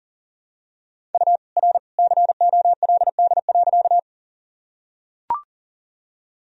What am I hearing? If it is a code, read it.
Morse code, heard as a keyed tone, about 40 wpm, 700 Hz. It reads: U R COLD.
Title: Sentences from Top 400 Words and Abbreviations 40wpm